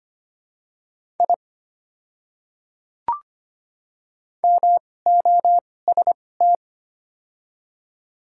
Morse code, heard as a keyed tone, about 25 wpm, spaced as for 18 wpm, 700 Hz. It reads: I MOST
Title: 2x Character Spacing - Top 100 Words - 25wpm